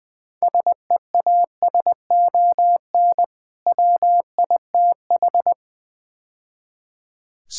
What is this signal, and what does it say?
Morse code, heard as a keyed tone, about 21 wpm, 700 Hz. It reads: SEASON WITH
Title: Build A Phrase - 5-Word Phrases - 20wpm